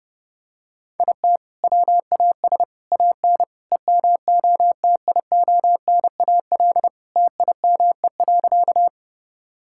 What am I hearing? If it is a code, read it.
Morse code, heard as a keyed tone, about 29 wpm, 700 Hz. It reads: IT WAS AN EMOTIONAL TIME.
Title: Sentences from Top 1000 Words - Review Part 5 - 30wpm